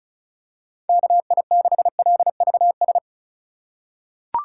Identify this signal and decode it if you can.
Morse code, heard as a keyed tone, about 36 wpm, 700 Hz. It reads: KI6LVS
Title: USA Call Signs - 2x Character Spacing - 35wpm